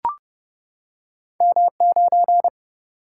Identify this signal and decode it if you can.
Morse code, heard as a keyed tone, about 30 wpm, 700 Hz. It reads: M9
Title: Single-Letter Single-Number 30wpm